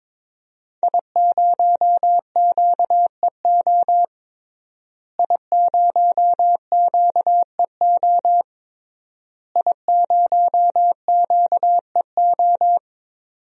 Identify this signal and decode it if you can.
Morse code, heard as a keyed tone, about 22 wpm, 700 Hz. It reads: I0QEO I0QEO I0QEO
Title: DXCC-100 Call Signs - 5 Characters - Repeated 3x - 22wpm